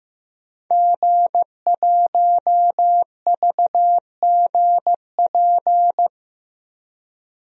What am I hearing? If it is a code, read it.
Morse code, heard as a keyed tone, about 15 wpm, 700 Hz. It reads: G1VGP
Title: DXCC-100 Call Signs - 5 Characters - 15wpm